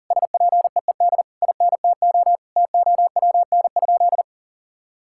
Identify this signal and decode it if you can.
Morse code, heard as a keyed tone, about 40 wpm, 700 Hz. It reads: SPEED INTO TOWN?
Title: Sentences from Top 1000 Words - Review Part 5 - 40wpm